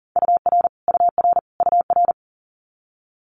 Morse code, heard as a keyed tone, about 43 wpm, 700 Hz. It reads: UR UR UR
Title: Multi-letter Phonemes Repeated 3x 40wpm